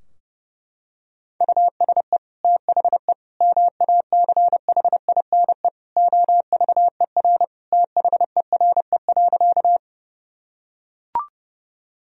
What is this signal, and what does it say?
Morse code, heard as a keyed tone, about 32 wpm, 700 Hz. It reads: USE THE MACHINE OVER THERE.
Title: Sentences from Top 1000 Words - Review Part 2 - 30wpm